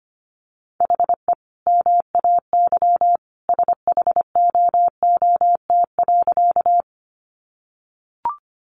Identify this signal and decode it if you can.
Morse code, heard as a keyed tone, about 25 wpm, 700 Hz. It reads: HE MAY SHOOT.
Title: Sentences from Top 1200 Words 25wpm